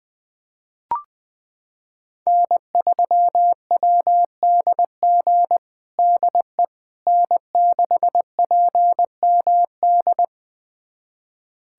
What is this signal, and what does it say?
Morse code, heard as a keyed tone, about 20 wpm, 700 Hz. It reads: N3WDG DE N6PMD
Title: USA Callsign DE Callsign - 5 Characters - 20wpm